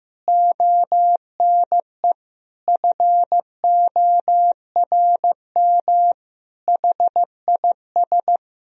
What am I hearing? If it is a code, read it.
Morse code, heard as a keyed tone, about 15 wpm, 700 Hz. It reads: ONE FORM HIS
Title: Sentences from Top 1000 Words - Review Part 1 - 15wpm